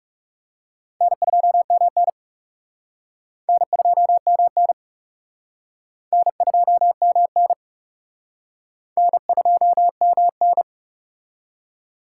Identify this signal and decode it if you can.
Morse code, heard as a keyed tone, about 39 wpm, 700 Hz. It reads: N2MN N2MN N2MN N2MN